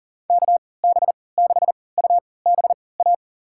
Morse code, heard as a keyed tone, about 38 wpm, spaced as for 22 wpm, 700 Hz. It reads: KB6UBA
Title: USA Call Signs - 6 Characters - 3x Character Spacing - 40wpm